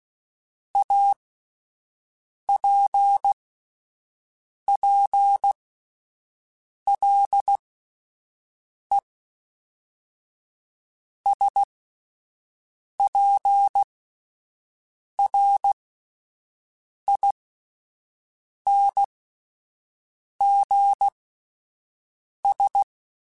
Morse code, heard as a keyed tone, about 16 wpm, spaced as for 5 wpm, 800 Hz. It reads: APPLE SPRINGS
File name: fromSMs / FromSMs mp3s / Cities@5